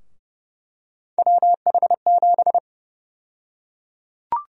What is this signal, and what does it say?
Morse code, heard as a keyed tone, about 30 wpm, 700 Hz. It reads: WH7
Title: USA Call Sign Prefixes - Repeated 2x - 30wpm